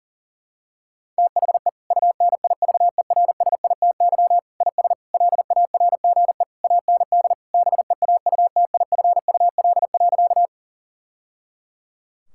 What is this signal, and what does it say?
Morse code, heard as a keyed tone, about 40 wpm, 690 Hz. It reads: THE UNIVERSITY IS LARGE AND BEAUTIFUL.
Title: Sentences from Top 1000 Words - Review Part 5 - 40wpm